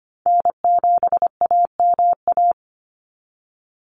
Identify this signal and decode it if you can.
Morse code, heard as a keyed tone, about 25 wpm, 700 Hz. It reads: N7AMA
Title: NA SOTA Chaser Callsigns 25wpm